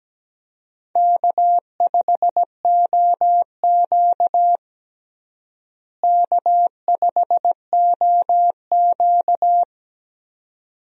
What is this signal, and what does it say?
Morse code, heard as a keyed tone, about 17 wpm, 700 Hz. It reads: K5OQ K5OQ
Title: USA Call Signs - 4 Characters - Repeated 2x - 17wpm